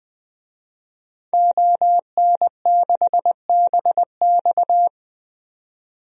Morse code, heard as a keyed tone, about 20 wpm, 700 Hz. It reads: ON6BX
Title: DXCC-100 Call Signs - 5 Characters - 20wpm